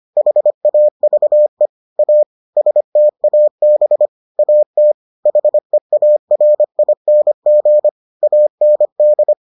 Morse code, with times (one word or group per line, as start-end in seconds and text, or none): 0.2-1.7 HAVE
2.0-2.2 A
2.6-4.1 STAB
4.4-4.9 AT
5.3-7.9 HEARING
8.2-9.3 AND